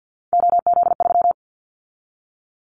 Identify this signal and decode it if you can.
Morse code, heard as a keyed tone, about 50 wpm, 700 Hz. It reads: OZ3
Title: Callsign Prefixes for Top 100 DXCC - 50wpm